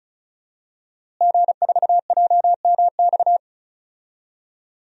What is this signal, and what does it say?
Morse code, heard as a keyed tone, about 36 wpm, 700 Hz. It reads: G4JMX
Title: DXCC-100 Call Signs - 5 Characters - Repeated 3x - 35wpm